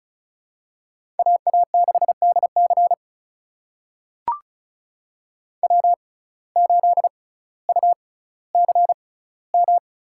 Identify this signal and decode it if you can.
Morse code, heard as a keyed tone, about 36 wpm, spaced as for 18 wpm, 700 Hz. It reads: AA6DC W8UCM
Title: USA Call Signs - 5 Characters - 6x Character Spacing - 35wpm